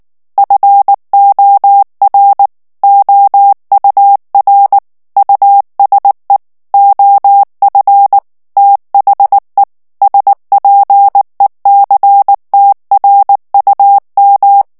Morse code, heard as a keyed tone, about 19 wpm, 800 Hz. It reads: FOR OUR USE OF THE SPECTRUM
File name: ANGLAIS texte clair 3 vitesse 18 mots minute